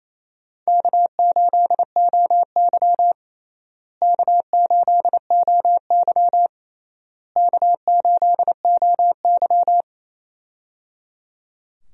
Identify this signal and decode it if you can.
Morse code, heard as a keyed tone, about 28 wpm, 700 Hz. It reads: K8OY K8OY K8OY
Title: USA Call Signs - 4 Characters - Repeated 3x - 28wpm